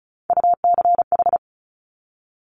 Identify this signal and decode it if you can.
Morse code, heard as a keyed tone, about 36 wpm, 700 Hz. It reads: UCH
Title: Top Three Letter Combinations in Words 35wpm